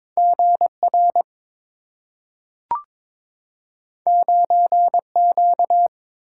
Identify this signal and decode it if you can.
Morse code, heard as a keyed tone, about 22 wpm, 700 Hz. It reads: GR 9Q
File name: practice / 22wpm / Two Letter-Number Combinations - 22wpm